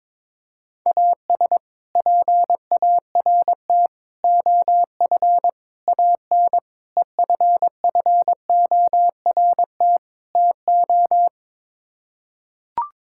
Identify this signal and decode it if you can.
Morse code, heard as a keyed tone, about 22 wpm, 700 Hz. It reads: AS PART OF AN EFFORT TO
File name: practice / 22wpm / Sets of 6 Words - 22wpm